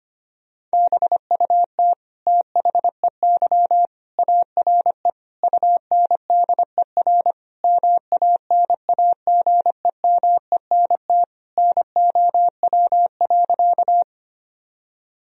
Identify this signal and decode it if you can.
Morse code, heard as a keyed tone, about 25 wpm, 700 Hz. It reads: BUT THEY ARE UNDER MANAGEMENT NOW.
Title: Sentences from Top 1000 Words - Review Part 4 - 25wpm